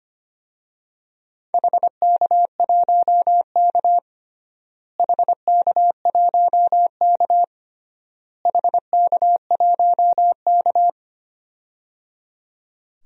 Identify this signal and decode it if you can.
Morse code, heard as a keyed tone, about 25 wpm, 700 Hz. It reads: HK1K HK1K HK1K
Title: DXCC-100 Call Signs - 4 Characters - Repeated 3x - 25wpm